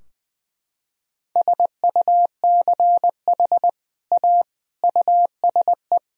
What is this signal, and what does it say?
Morse code, heard as a keyed tone, about 21 wpm, 700 Hz. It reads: SUCH A USE
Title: Sentences from Top 1000 Words - Review Part 3 - 20wpm